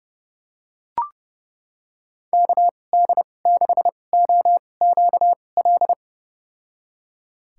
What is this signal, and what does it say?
Morse code, heard as a keyed tone, about 30 wpm, spaced as for 22 wpm, 700 Hz. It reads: KD6OQL
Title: USA Call Signs - 6 Characters - 2x Character Spacing - 30wpm